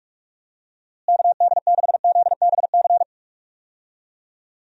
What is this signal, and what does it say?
Morse code, heard as a keyed tone, about 47 wpm, 700 Hz. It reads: KD6ZBC
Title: USA Call Signs - 6 Characters - Repeated 3x - 45wpm